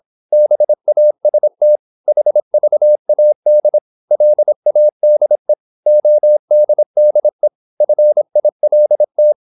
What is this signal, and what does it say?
Morse code, heard as a keyed tone, about 26 wpm, 600 Hz. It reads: BAST HVAD LADE ODDE FILT
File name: Korte ord DK 26wpm